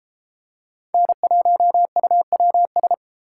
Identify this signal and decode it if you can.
Morse code, heard as a keyed tone, about 31 wpm, 700 Hz. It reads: N1UWS